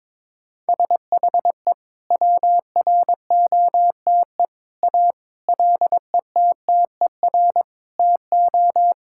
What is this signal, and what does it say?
Morse code, heard as a keyed tone, about 22 wpm, 700 Hz. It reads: SHE WROTE A LETTER TO
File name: practice / 22wpm / Sentences with 2-Word Phrases - 22wpm